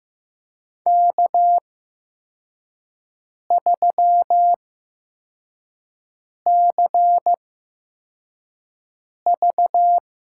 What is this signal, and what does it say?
Morse code, heard as a keyed tone, about 15 wpm, spaced as for 4 wpm, 700 Hz. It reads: K3CV
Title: USA Call Signs - 4 Characters - 8x Character Spacing - 15wpm